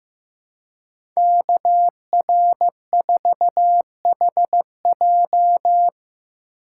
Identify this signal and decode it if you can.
Morse code, heard as a keyed tone, about 15 wpm, 700 Hz. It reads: KR4HJ
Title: USA Call Signs - 5 Characters - 3x Character Spacing - 15wpm